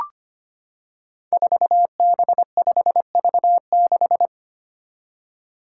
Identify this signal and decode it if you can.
Morse code, heard as a keyed tone, about 25 wpm, 700 Hz. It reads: 4B5V6
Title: Five Challenging Letter-Number Combinations 25wpm